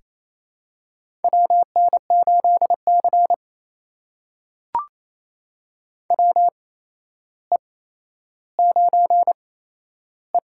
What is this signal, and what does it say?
Morse code, heard as a keyed tone, about 28 wpm, spaced as for 12 wpm, 700 Hz. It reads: WN8C WE9E
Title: USA Call Signs - 4 Characters - 8x Character Spacing - 28wpm